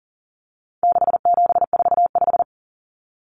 Morse code, heard as a keyed tone, about 40 wpm, 700 Hz. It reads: 6745
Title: Common Cut Numbers - 4-Digit 40wpm